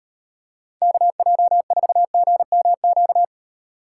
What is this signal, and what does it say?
Morse code, heard as a keyed tone, about 38 wpm, 700 Hz. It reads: KJ4GMQ